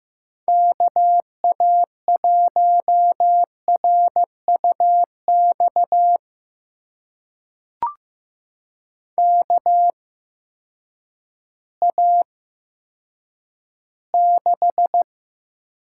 Morse code, heard as a keyed tone, about 15 wpm, spaced as for 9 wpm, 700 Hz. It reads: KA1RUX KA6
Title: USA Call Signs - 6 Characters - 8x Character Spacing - 15wpm